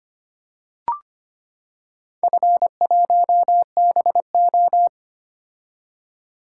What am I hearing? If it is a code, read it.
Morse code, heard as a keyed tone, about 25 wpm, 700 Hz. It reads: F1BO
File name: practice / 25wpm / Four Letter-Number Combinations - 25wpm